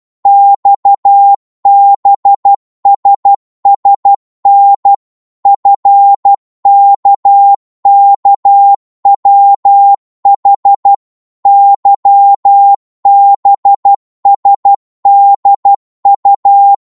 Morse code, heard as a keyed tone, about 12 wpm, 800 Hz. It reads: XBSSN FKKWH YBSDU
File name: Morse / ltr-12wpm